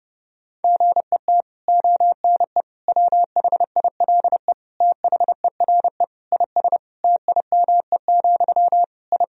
Morse code, heard as a keyed tone, about 32 wpm, 700 Hz. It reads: GET ONE WHILE THERE IS TIME, I
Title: Sentences from Top 1000 Words - Review Part 1 - 30wpm